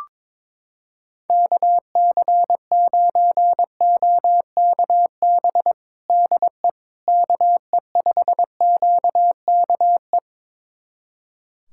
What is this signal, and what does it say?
Morse code, heard as a keyed tone, about 22 wpm, 700 Hz. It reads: KC9OKB DE KE5QKE
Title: USA Callsign DE Callsign - 6 Characters - 22wpm